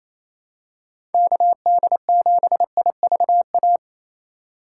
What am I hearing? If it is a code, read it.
Morse code, heard as a keyed tone, about 28 wpm, 700 Hz. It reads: KD7IVA
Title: USA Call Signs - 2x Character Spacing - 28wpm